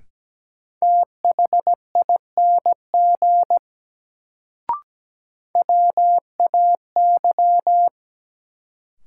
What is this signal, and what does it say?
Morse code, heard as a keyed tone, about 17 wpm, 700 Hz. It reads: THING WAY
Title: Sentences from Top 100 Words - New Word Drill - 17wpm